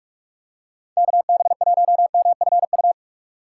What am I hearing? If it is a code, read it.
Morse code, heard as a keyed tone, about 46 wpm, 700 Hz. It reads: KB1MFU
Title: USA Call Signs - 6 Characters - 2x Character Spacing - 45wpm